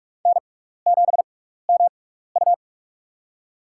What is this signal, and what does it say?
Morse code, heard as a keyed tone, about 46 wpm, spaced as for 16 wpm, 700 Hz. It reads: N7MU
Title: USA Call Signs - 4 Characters - 6x Character Spacing - 45wpm